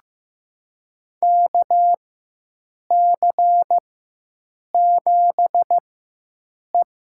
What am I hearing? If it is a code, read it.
Morse code, heard as a keyed tone, about 15 wpm, spaced as for 7 wpm, 700 Hz. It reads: KC7E
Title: USA Call Signs - 4 Characters - 4x Character Spacing - 15wpm